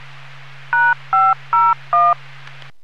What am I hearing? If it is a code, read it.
DTMF, heard as telephone touch tones, tones of about 220 ms. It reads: #501